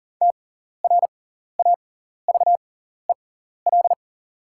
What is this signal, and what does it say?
Morse code, heard as a keyed tone, about 40 wpm, spaced as for 14 wpm, 700 Hz. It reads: TRAVEL
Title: 6x Character Spacing - Top 500 Words - 40wpm